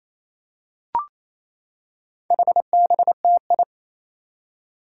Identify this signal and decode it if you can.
Morse code, heard as a keyed tone, about 28 wpm, 700 Hz. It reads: HBTI
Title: Four Challenging Letter-Number Combinations 28wpm